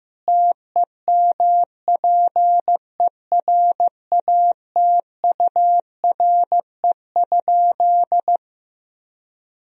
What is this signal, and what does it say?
Morse code, heard as a keyed tone, about 15 wpm, 700 Hz. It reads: TEMPERATURE?
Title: Sentences from Top 1000 Words - Review Part 4 - 15wpm